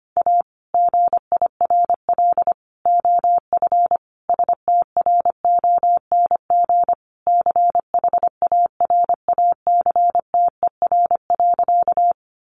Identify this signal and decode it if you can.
Morse code, heard as a keyed tone, about 25 wpm, 700 Hz. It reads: A GIRL OF STRONG CHARACTER.
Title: Sentences from Top 1200 Words 25wpm